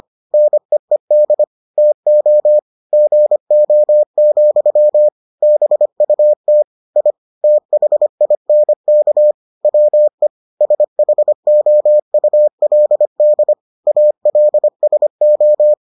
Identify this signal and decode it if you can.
Morse code, heard as a keyed tone, about 25 wpm, 600 Hz. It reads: NEED TO GO, BUT I THINK WE SHOULD ALSO